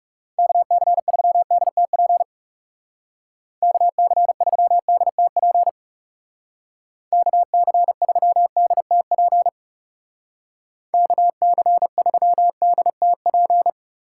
Morse code, heard as a keyed tone, about 36 wpm, 700 Hz. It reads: KC3DTP KC3DTP KC3DTP KC3DTP
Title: Speed Racing - USA Call Signs 30wpm